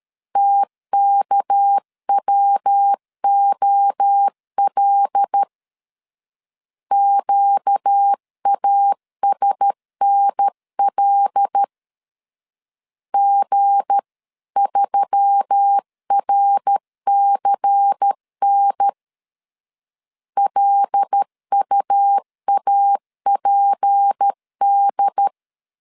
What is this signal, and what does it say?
Morse code, heard as a keyed tone, about 13 wpm, 800 Hz. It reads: TKWOL QASNL G3RCN LUAPD
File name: LEZIONI / 7 GF9Z / 7 G 12wpm 2055